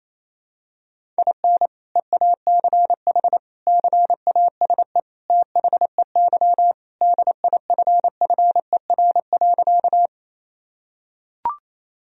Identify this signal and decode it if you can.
Morse code, heard as a keyed tone, about 28 wpm, 700 Hz. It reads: IN EACH CASE THEY DIFFER.
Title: Sentences from Top 1000 Words - Review Part 1 - 28wpm